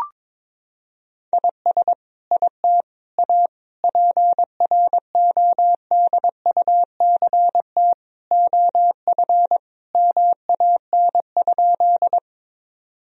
Morse code, heard as a keyed tone, about 22 wpm, 700 Hz. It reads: IS IT A PRODUCT OF MAN?
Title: Sentences from Top 1000 Words - Review Part 2 - 22wpm